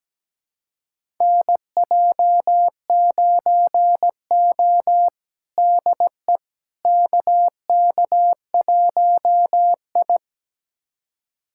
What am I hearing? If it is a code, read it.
Morse code, heard as a keyed tone, about 17 wpm, 700 Hz. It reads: NJ9O DE KK1I
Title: USA Callsign DE Callsign - 4 Characters - 17wpm